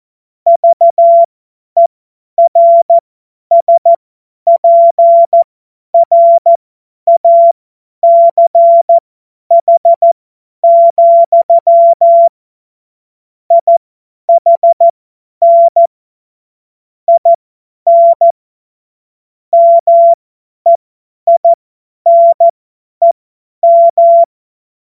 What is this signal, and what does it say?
Morse code, heard as a keyed tone, about 14 wpm, spaced as for 10 wpm, 680 Hz. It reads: VERSPRACH, IHN IN MEINEM
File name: Morsekurs / Witze / Witz 17. Mai 2022 50 F70